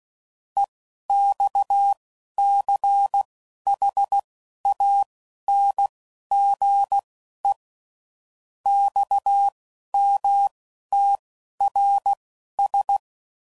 Morse code, heard as a keyed tone, about 16 wpm, spaced as for 11 wpm, 800 Hz. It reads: EXCHANGE XMTRS
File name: CW Code Groups @10WPM / QSOs / QSO WA7QOX